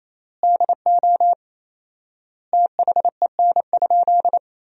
Morse code, heard as a keyed tone, about 28 wpm, 700 Hz. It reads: DO THEN?
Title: 4x Word-Spacing - Sentences from Top 100 Words - 28wpm